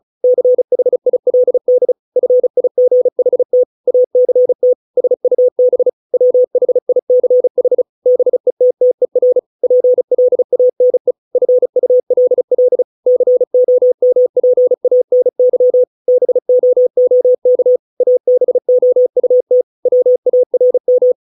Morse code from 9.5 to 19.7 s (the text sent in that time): PLANE FULL COMPANY BOOK ABOUT